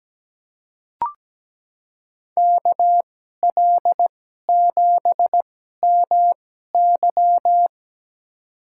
Morse code, heard as a keyed tone, about 17 wpm, spaced as for 12 wpm, 700 Hz. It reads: KL7MY
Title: USA Call Signs - 5 Characters - 2x Character Spacing - 17wpm